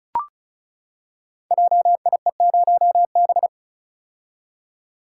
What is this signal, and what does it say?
Morse code, heard as a keyed tone, about 36 wpm, 690 Hz. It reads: JIE0B
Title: Code Groups - 5 Letter:Numbers - 35wpm